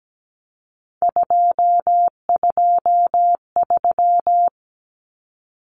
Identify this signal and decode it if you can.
Morse code, heard as a keyed tone, about 17 wpm, 700 Hz. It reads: 223
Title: Common Cut Numbers - 3-Digit 17wpm